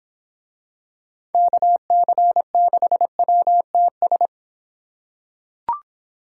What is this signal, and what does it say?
Morse code, heard as a keyed tone, about 24 wpm, 700 Hz. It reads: KC6WTS